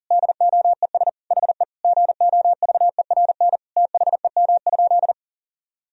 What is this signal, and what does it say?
Morse code, heard as a keyed tone, about 40 wpm, 700 Hz. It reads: DOES HE GOVERN THEM?
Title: Sentences from Top 1000 Words - Review Part 2 - 40wpm